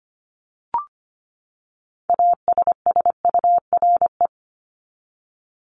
Morse code, heard as a keyed tone, about 25 wpm, 700 Hz. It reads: ASSURE
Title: Top 1501-1750 Words 25wpm